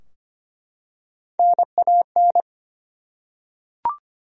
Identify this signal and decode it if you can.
Morse code, heard as a keyed tone, about 25 wpm, 700 Hz. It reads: NAN
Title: USA Common Names - Max 3 Letters - 25wpm